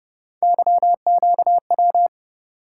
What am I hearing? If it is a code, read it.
Morse code, heard as a keyed tone, about 30 wpm, 700 Hz. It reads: YQW